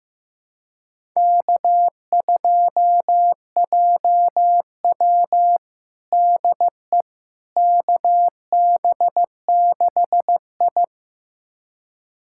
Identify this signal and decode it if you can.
Morse code, heard as a keyed tone, about 15 wpm, 700 Hz. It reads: K2JW DE KB6I
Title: USA Callsign DE Callsign - 4 Characters - 15wpm